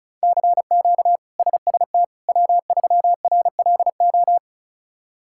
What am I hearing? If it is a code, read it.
Morse code, heard as a keyed tone, about 36 wpm, 700 Hz. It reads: CQ SST W3RLO
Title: K1USN SST QSO Practice - 35wpm